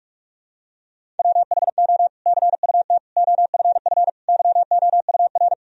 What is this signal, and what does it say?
Morse code, heard as a keyed tone, about 45 wpm, 700 Hz. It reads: WHO CUT OFF YOUR
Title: Sentences from Top 1000 Words - Review Part 1 - 45wpm